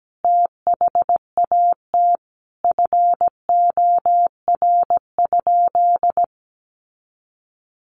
Morse code, heard as a keyed tone, about 17 wpm, 700 Hz. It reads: THAT FOR?
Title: Sentences from Top 700 Words 17wpm